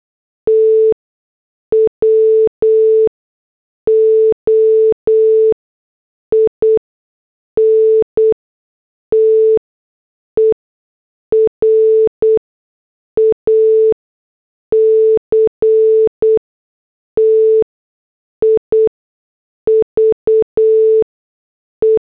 Morse code, heard as a keyed tone, about 8 wpm, spaced as for 6 wpm, 440 Hz. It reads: TWO INTERACTIVE
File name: morse / headlines